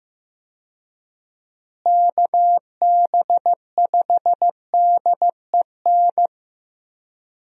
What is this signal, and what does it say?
Morse code, heard as a keyed tone, about 15 wpm, 700 Hz. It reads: KB5DEN
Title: USA Call Signs - 6 Characters - Repeated 2x - 15wpm